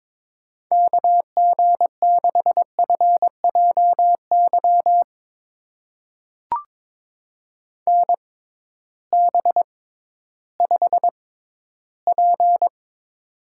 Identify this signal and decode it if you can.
Morse code, heard as a keyed tone, about 22 wpm, spaced as for 13 wpm, 700 Hz. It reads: KG6FJY NB5P
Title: USA Call Signs - 6x Character Spacing - 22wpm